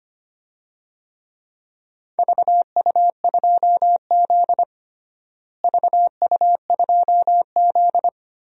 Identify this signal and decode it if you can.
Morse code, heard as a keyed tone, about 25 wpm, 700 Hz. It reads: VU2Z VU2Z
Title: DXCC-100 Call Signs - 4 Characters - Repeated 3x - 25wpm